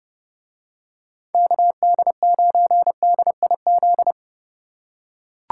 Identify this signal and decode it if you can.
Morse code, heard as a keyed tone, about 29 wpm, 700 Hz. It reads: KD9DIZ